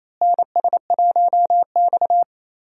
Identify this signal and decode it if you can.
Morse code, heard as a keyed tone, about 28 wpm, 700 Hz. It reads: NS1X
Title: USA Call Signs - 4 Characters - Repeated 2x - 28wpm